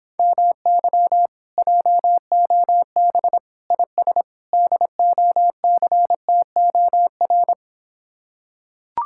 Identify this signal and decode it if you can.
Morse code, heard as a keyed tone, about 26 wpm, 700 Hz. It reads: MY JOB IS DOCTOR